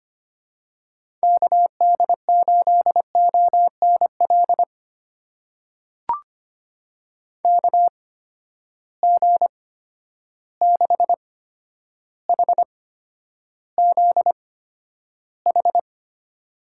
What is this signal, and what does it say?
Morse code, heard as a keyed tone, about 25 wpm, spaced as for 11 wpm, 700 Hz. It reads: KD8ONL KG6HZH
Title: USA Call Signs - 6 Characters - 8x Character Spacing - 25wpm